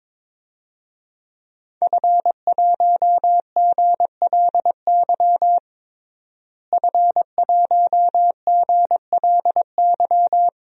Morse code, heard as a keyed tone, about 22 wpm, 700 Hz. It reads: F1GLY F1GLY
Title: DXCC-100 Call Signs - 5 Characters - Repeated 2x - 22wpm